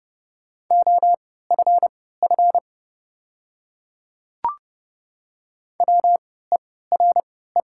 Morse code, heard as a keyed tone, about 30 wpm, spaced as for 17 wpm, 700 Hz. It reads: OFF WERE
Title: 3x Character Spacing - Top 300 Words - 30wpm